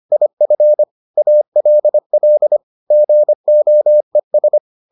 Morse code, heard as a keyed tone, about 25 wpm, 600 Hz. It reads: IF ALL GOES